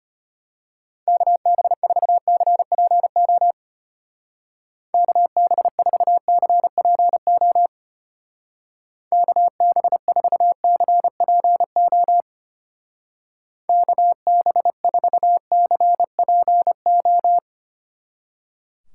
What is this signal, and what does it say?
Morse code, heard as a keyed tone, about 32 wpm, 700 Hz. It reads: KB4CPO KB4CPO KB4CPO KB4CPO